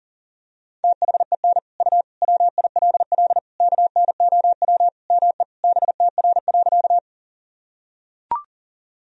Morse code, heard as a keyed tone, about 40 wpm, 700 Hz. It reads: THEN U WILL KNOW ME BTR.
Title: Sentences from Top 400 Words and Abbreviations 40wpm